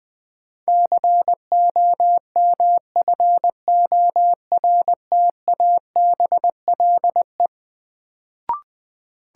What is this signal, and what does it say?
Morse code, heard as a keyed tone, about 20 wpm, 700 Hz. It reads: COMFORTABLE
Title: Sentences from Top 1000 Words - Review Part 7 - New Word Drill - 20wpm